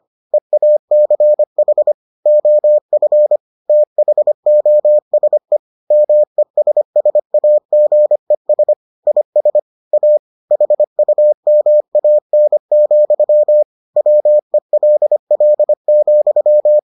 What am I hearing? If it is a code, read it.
Morse code, heard as a keyed tone, about 25 wpm, 600 Hz. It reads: EACH OF THOSE MESSAGES IS A HUMAN, WELL,